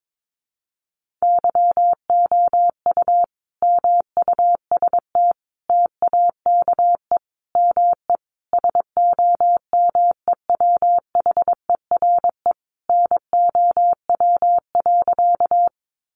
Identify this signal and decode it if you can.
Morse code, heard as a keyed tone, about 22 wpm, 700 Hz. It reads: YOU MUST TAKE ME SOMEWHERE NOW.
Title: Sentences from Top 1400 Words 22wpm